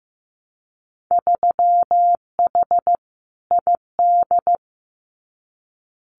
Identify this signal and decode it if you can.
Morse code, heard as a keyed tone, about 15 wpm, 700 Hz. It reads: 3H ID
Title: Winter Field Day Exchanges Encore 15wpm